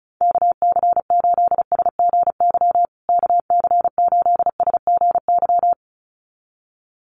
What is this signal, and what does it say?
Morse code, heard as a keyed tone, about 34 wpm, 700 Hz. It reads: KC8SGY KC8SGY
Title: USA Call Signs Repeated 2x 35wpm